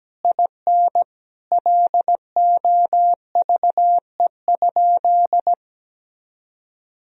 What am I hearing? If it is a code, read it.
Morse code, heard as a keyed tone, about 17 wpm, 700 Hz. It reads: IN LOVE?
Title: Sentences from Top 1000 Words - Review Part 2 - 17wpm